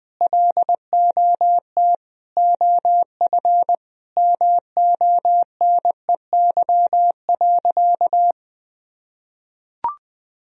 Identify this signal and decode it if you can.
Morse code, heard as a keyed tone, about 20 wpm, 700 Hz. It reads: LOT OF MONEY.
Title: Sentences from Top 1000 Words - Review Part 4 - 20wpm